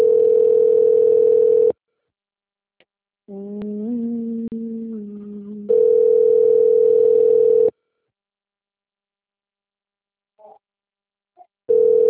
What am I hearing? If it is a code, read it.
Ringback tone (North America), 440 + 480 Hz.
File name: out- - - - - .2949